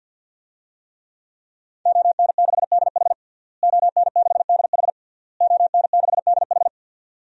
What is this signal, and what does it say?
Morse code, heard as a keyed tone, about 46 wpm, 700 Hz. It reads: ON6DH ON6DH ON6DH
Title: DXCC-100 Call Signs - 5 Characters - Repeated 3x - 50wpm